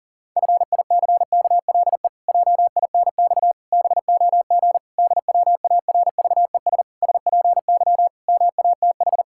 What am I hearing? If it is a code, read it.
Morse code, heard as a keyed tone, about 40 wpm, 700 Hz. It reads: FICKLE JINX BOG DWARVES SPY MATH
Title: Pangram Challenge v2 - 40wpm